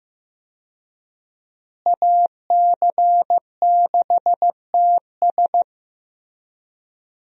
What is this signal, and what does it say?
Morse code, heard as a keyed tone, about 15 wpm, 700 Hz. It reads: AC6TS
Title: USA Call Signs - 5 Characters - Repeated 2x - 15wpm